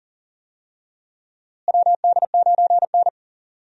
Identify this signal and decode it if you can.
Morse code, heard as a keyed tone, about 40 wpm, 700 Hz. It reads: WD9N
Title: USA Call Signs - 4 Characters - Repeated 3x - 40wpm